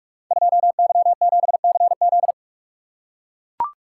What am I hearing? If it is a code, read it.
Morse code, heard as a keyed tone, about 42 wpm, 700 Hz. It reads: 2Y7CZ
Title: Code Groups - 5 Letter:Numbers - 45wpm